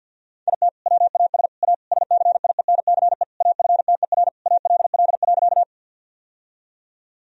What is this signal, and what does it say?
Morse code, heard as a keyed tone, about 52 wpm, 700 Hz. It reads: IT WAS A SCIENCE AFTER ALL.
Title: Sentences from Top 1000 Words - Review Part 2 - 50wpm